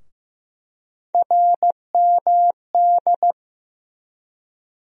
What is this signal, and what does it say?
Morse code, heard as a keyed tone, about 15 wpm, 700 Hz. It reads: RMD
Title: USA Call Sign Postfixes - 3 Letters 15wpm